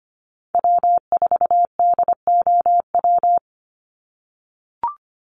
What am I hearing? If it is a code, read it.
Morse code, heard as a keyed tone, about 25 wpm, 700 Hz. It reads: W4DOW
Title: NA SOTA Chaser Callsigns 25wpm